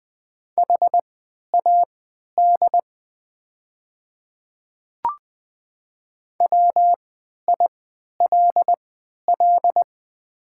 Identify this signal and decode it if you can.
Morse code, heard as a keyed tone, about 20 wpm, spaced as for 11 wpm, 700 Hz. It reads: HAD WILL
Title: 3x Character Spacing - Top 100 Words - 20wpm